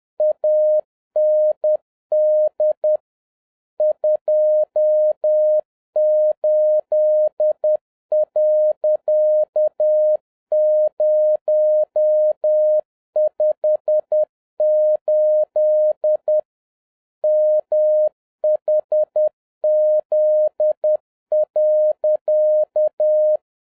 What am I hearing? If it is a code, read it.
Morse code, heard as a keyed tone, about 10 wpm, 610 Hz. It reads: AND 28.058 MHZ.